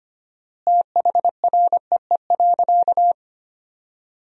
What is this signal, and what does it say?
Morse code, heard as a keyed tone, about 25 wpm, 700 Hz. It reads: THREE.
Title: 4x Word-Spacing - Sentences from Top 200 Words - 25wpm